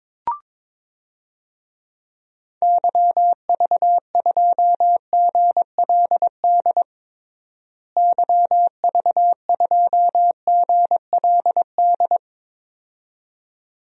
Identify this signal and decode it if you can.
Morse code, heard as a keyed tone, about 22 wpm, 700 Hz. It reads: YV2GLD YV2GLD
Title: DXCC-100 Call Signs - 6 Characters - Repeated 2x - 22wpm